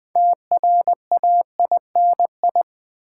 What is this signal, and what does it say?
Morse code, heard as a keyed tone, about 20 wpm, 700 Hz. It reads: TRAINI
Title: Build A Word - 2 Letters At A Time - 8-Letter Words - 20wpm